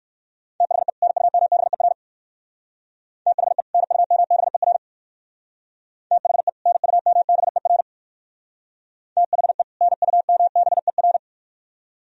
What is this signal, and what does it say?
Morse code, heard as a keyed tone, about 59 wpm, 700 Hz. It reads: THE NUMBER THE NUMBER THE NUMBER THE NUMBER